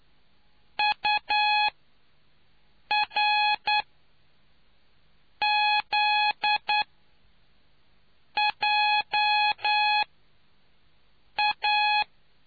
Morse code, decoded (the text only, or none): URZJA